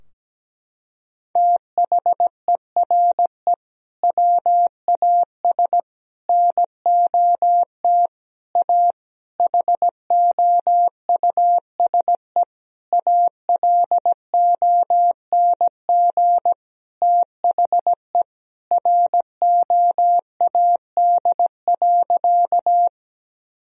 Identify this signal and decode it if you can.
Morse code, heard as a keyed tone, about 17 wpm, 700 Hz. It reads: THERE WAS NOT A HOUSE ALONG THE ROAD.
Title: Sentences from Top 1000 Words - Review Part 2 - 17wpm